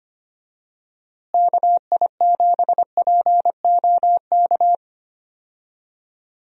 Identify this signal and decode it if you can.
Morse code, heard as a keyed tone, about 25 wpm, 700 Hz. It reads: KI7POK
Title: USA Call Signs - 6 Characters - Repeated 3x - 25wpm